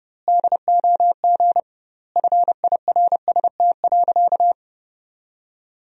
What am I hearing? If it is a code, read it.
Morse code, heard as a keyed tone, about 30 wpm, 700 Hz. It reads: DOG FIRST.
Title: 2x Word-Spacing - Sentences from Top 400 Words - 30wpm